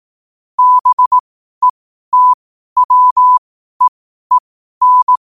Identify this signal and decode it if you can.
Morse code, heard as a keyed tone, about 18 wpm, spaced as for 12 wpm, 1000 Hz. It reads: BETWEEN
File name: qotdfiles / 12wpm